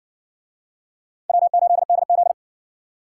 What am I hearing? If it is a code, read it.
Morse code, heard as a keyed tone, about 53 wpm, 700 Hz. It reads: W8DZ